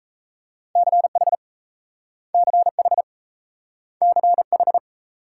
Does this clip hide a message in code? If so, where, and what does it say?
Yes: Morse code from 0.8 to 4.8 s, heard as a keyed tone, about 38 wpm, 700 Hz. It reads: CH CH CH